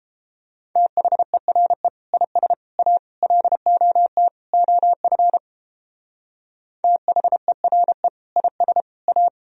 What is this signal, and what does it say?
Morse code, heard as a keyed tone, about 31 wpm, 700 Hz. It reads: THERE IS A LOT OF THERE IS A